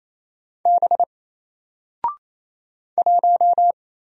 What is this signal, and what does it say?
Morse code, heard as a keyed tone, about 27 wpm, 700 Hz. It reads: B 1
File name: Single-Letter-Number-28wpm